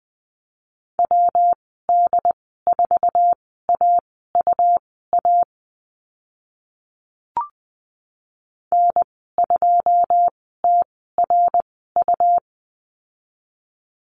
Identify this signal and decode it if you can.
Morse code, heard as a keyed tone, about 20 wpm, spaced as for 15 wpm, 700 Hz. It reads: WD4AUA N2TRU
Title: 154 Lesson 25g - Letter p - Call Signs 20wpm